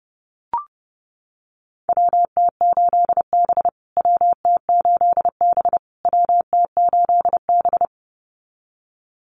Morse code, heard as a keyed tone, about 30 wpm, 700 Hz. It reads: WT8B WT8B WT8B
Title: USA Call Signs Repeated 3x 30wpm